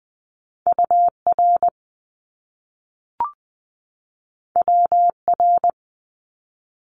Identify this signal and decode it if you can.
Morse code, heard as a keyed tone, about 20 wpm, 700 Hz. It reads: UR WR
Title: Top Two Letter Combinations in Words 20wpm